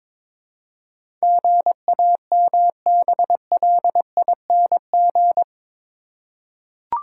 Morse code, heard as a keyed tone, about 22 wpm, 700 Hz. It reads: GAMBLING
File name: practice / 22wpm / Eight Letter Words - 22wpm